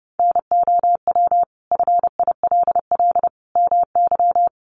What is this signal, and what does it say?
Morse code, heard as a keyed tone, about 30 wpm, 700 Hz. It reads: NOW FILL MY
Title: Sentences from Top 1500 Words 30wpm